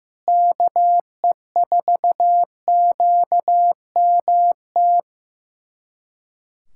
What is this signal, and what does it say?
Morse code, heard as a keyed tone, about 15 wpm, 700 Hz. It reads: KE4QMT
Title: USA Callsign DE Callsign - 6 Characters - 15wpm